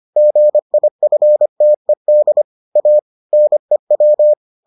Morse code, heard as a keyed tone, about 25 wpm, 600 Hz. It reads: GIFTED A NEW